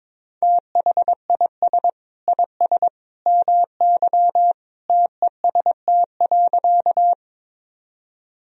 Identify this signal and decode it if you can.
Morse code, heard as a keyed tone, about 22 wpm, 700 Hz. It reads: THIS IS MY TEST.
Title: Sentences from Top 1000 Words - Review Part 3 - 22wpm